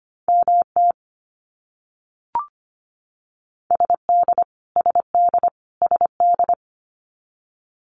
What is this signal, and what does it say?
Morse code, heard as a keyed tone, about 25 wpm, 700 Hz. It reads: MT SD SD SD
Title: US States Repeated 3x 25wpm